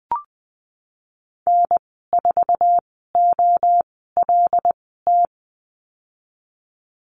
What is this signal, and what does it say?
Morse code, heard as a keyed tone, about 20 wpm, spaced as for 15 wpm, 700 Hz. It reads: N4OLT
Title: 076 Lesson 13g - Letter L - Call Signs 20wpm